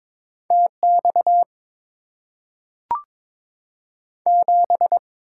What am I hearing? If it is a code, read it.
Morse code, heard as a keyed tone, about 22 wpm, 700 Hz. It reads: TX 7
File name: Top 90 QSO Elements - 6x Character Spacing - 22wpm